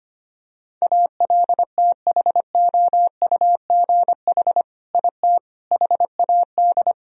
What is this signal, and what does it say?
Morse code, heard as a keyed tone, about 25 wpm, 700 Hz. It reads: ALTHOUGH IT HAD
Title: Sentences from Top 1000 Words - Review Part 3 - 25wpm